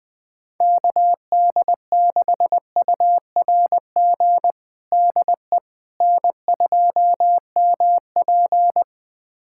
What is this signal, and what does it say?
Morse code, heard as a keyed tone, about 20 wpm, 690 Hz. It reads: KD6URG DE N2MP
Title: USA Callsign DE Callsign 20wpm